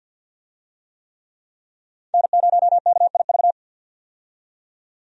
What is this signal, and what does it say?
Morse code, heard as a keyed tone, about 48 wpm, 700 Hz. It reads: N0KIV
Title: USA Call Signs - 5 Characters - 50wpm